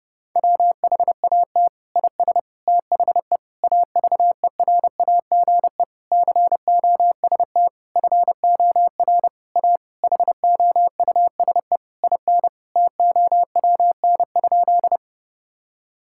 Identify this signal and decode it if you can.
Morse code, heard as a keyed tone, about 32 wpm, 700 Hz. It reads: WHAT IS THE AVERAGE COST FOR A HOUSE IN TOWN?
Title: Sentences from Top 1000 Words - Review Part 6 - 30wpm